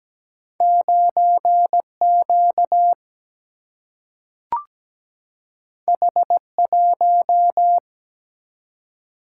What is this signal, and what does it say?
Morse code, heard as a keyed tone, about 17 wpm, 700 Hz. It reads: 9Q H1
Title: Single-Letter Single-Number 17wpm